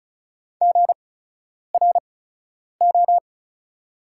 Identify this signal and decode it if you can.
Morse code, heard as a keyed tone, about 36 wpm, spaced as for 10 wpm, 690 Hz. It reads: GRO
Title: 8x Character Spacing - Top 500 Words - 35wpm